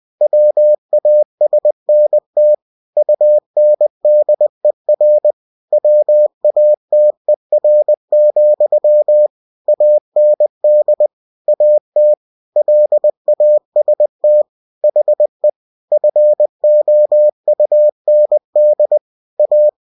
Morse code, heard as a keyed tone, about 20 wpm, 600 Hz. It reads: WASNT UNDER WATER, AND AT LAST HE FOUND A
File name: cw-books / winnie-the-pooh / winnie-20wpm-page-0092